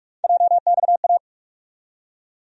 Morse code, heard as a keyed tone, about 46 wpm, 700 Hz. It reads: JXA
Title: USA Call Sign Postfixes - 3 Letters 45wpm